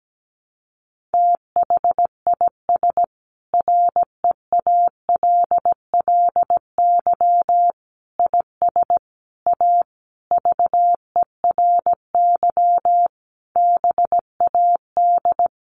Morse code, heard as a keyed tone, about 17 wpm, 700 Hz. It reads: THIS REALLY IS A VERY BAD
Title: Sentences from Top 700 Words 17wpm